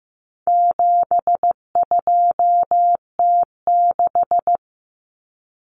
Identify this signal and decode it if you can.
Morse code, heard as a keyed tone, about 15 wpm, 700 Hz. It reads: 72T6
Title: Common Cut Numbers - 4-Digit 15wpm